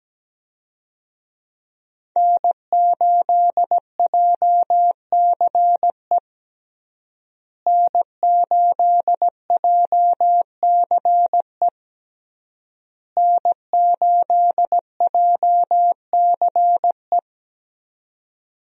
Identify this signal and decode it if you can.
Morse code, heard as a keyed tone, about 17 wpm, 700 Hz. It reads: N8JCE N8JCE N8JCE
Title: USA Call Signs - 5 Characters - Repeated 3x - 17wpm